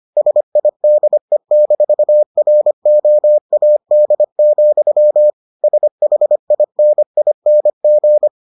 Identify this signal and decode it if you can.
Morse code, heard as a keyed tone, about 25 wpm, 600 Hz. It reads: SIDE-ROAD, SHINING